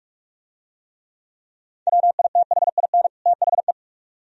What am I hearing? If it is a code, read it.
Morse code, heard as a keyed tone, about 41 wpm, 700 Hz. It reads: WITHIN THE